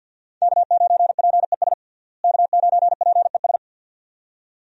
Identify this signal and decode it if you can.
Morse code, heard as a keyed tone, about 52 wpm, 700 Hz. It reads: K9PES K9PES
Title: USA Call Signs - 5 Characters - Repeated 2x - 50wpm